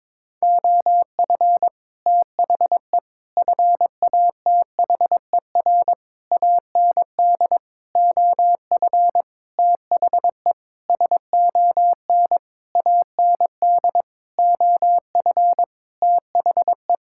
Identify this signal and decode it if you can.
Morse code, heard as a keyed tone, about 22 wpm, 700 Hz. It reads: OF THE FATHER AND OF THE SON AND OF THE
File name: Sentences with 9-Word Phrases - 22wpm